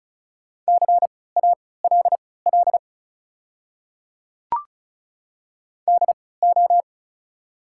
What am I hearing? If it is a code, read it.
Morse code, heard as a keyed tone, about 36 wpm, spaced as for 20 wpm, 700 Hz. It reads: CALL DO
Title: 3x Character Spacing - Top 100 Words - 35wpm